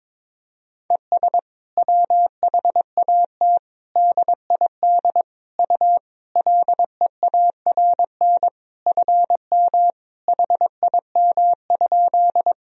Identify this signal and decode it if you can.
Morse code, heard as a keyed tone, about 22 wpm, 700 Hz. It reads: ES WHAT DID U LEARN FM HIM?
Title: Sentences from Top 300 Words and Abbreviations 22wpm